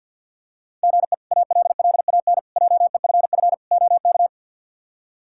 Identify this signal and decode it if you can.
Morse code, heard as a keyed tone, about 52 wpm, 700 Hz. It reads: GE ALLAN JEFF OK
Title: K1USN SST QSO Practice - 50wpm